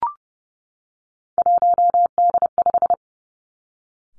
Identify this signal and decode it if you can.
Morse code, heard as a keyed tone, about 30 wpm, 700 Hz. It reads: 1D5
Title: Three Challenging Letter-Number Combinations 30wpm